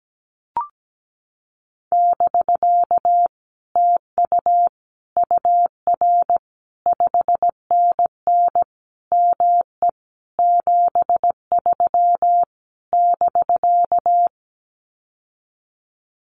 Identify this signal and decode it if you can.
Morse code, heard as a keyed tone, about 17 wpm, 700 Hz. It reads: <BK> TU UR 5NN ME 73 <BK>
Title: Common QSO Phrases Encore 17wpm